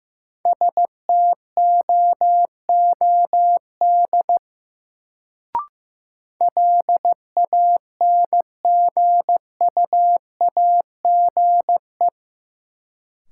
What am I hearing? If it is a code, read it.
Morse code, heard as a keyed tone, about 15 wpm, 700 Hz. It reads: STOOD LANGUAGE
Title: Sentences from Top 500 Words - New Word Drill - 15wpm